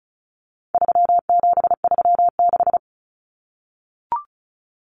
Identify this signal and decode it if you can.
Morse code, heard as a keyed tone, about 34 wpm, 700 Hz. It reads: 3736
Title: Numbers - 4-Digit 35wpm